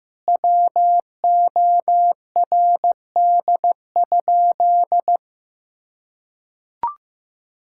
Morse code, heard as a keyed tone, about 15 wpm, 700 Hz. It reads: WORD?
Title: Sentences from Top 1000 Words - Review Part 6 - 15wpm